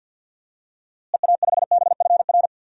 Morse code, heard as a keyed tone, about 47 wpm, 700 Hz. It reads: EA5BFR
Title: DXCC-100 Call Signs - 6 Characters - Repeated 3x - 50wpm